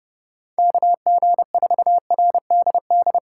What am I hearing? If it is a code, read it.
Morse code, heard as a keyed tone, about 32 wpm, 700 Hz. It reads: KG4RDD
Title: USA Call Signs - 6 Characters - 4x Character Spacing - 30wpm